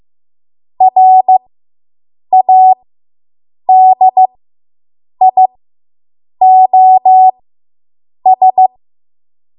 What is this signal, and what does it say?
Morse code, heard as a keyed tone, about 15 wpm, spaced as for 7 wpm, 750 Hz. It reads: RADIOS